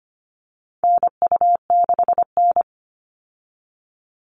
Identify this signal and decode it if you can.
Morse code, heard as a keyed tone, about 25 wpm, 700 Hz. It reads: NU6N
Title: Call Signs Repeated 3x - 25wpm